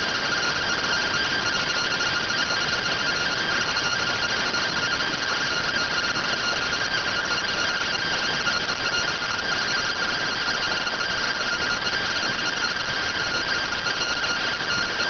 Two-tone teleprinter signal, about 170 Hz shift.